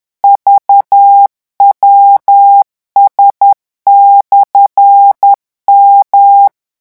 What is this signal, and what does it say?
Morse code, heard as a keyed tone, about 10 wpm, 800 Hz. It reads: VWS/M
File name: morse mp3 / revision13-vitesse 10 mots